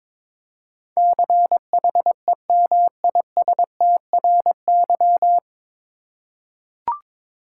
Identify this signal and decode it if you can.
Morse code, heard as a keyed tone, about 22 wpm, 700 Hz. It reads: CHEMISTRY
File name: 9 Letter Words - First 3 Letters Given - 22wpm